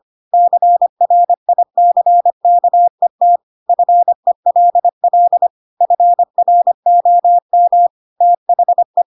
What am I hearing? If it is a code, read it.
Morse code, heard as a keyed tone, about 25 wpm, 700 Hz. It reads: CRICKET FELL FROM THE
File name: Pinocchio / Pinocchio ch4 25 25 ik1hge